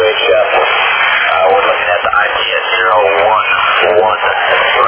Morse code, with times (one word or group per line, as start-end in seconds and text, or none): none